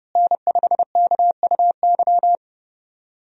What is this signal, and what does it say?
Morse code, heard as a keyed tone, about 31 wpm, 700 Hz. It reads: N5KUY
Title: USA Call Signs - 5 Characters - Repeated 3x - 30wpm